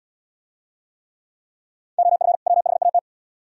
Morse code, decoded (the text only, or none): OF WHAT